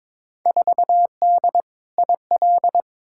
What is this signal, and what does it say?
Morse code, heard as a keyed tone, about 22 wpm, 700 Hz. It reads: 4D IL
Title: Field Day Exchanges Encore - 22wpm